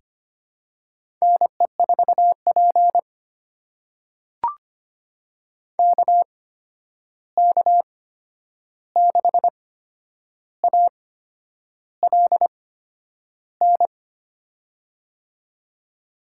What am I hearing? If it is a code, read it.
Morse code, heard as a keyed tone, about 25 wpm, spaced as for 9 wpm, 700 Hz. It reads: NE4P KK6ALN
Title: USA Call Signs - 8x Character Spacing - 25wpm